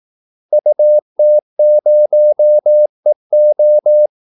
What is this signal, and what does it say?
Morse code, heard as a keyed tone, about 18 wpm, 600 Hz. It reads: UT0EO
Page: Kaldesignaler 18 wpm